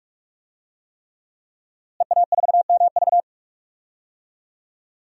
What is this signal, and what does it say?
Morse code, heard as a keyed tone, about 46 wpm, 700 Hz. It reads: EA4MV
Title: DXCC-100 Call Signs - 5 Characters - Repeated 3x - 45wpm